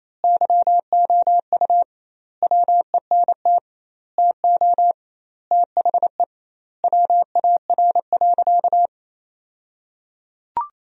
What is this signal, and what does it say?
Morse code, heard as a keyed tone, about 28 wpm, 700 Hz. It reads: YOU WENT TO THE WAR.
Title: 2x Word-Spacing - Sentences from Top 400 Words - 28wpm